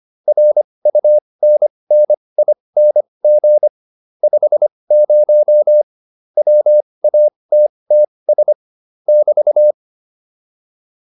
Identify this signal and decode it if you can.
Morse code, heard as a keyed tone, about 25 wpm, spaced as for 20 wpm, 600 Hz. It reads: RUNNING 50 WATTS <BT>
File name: QSO406 20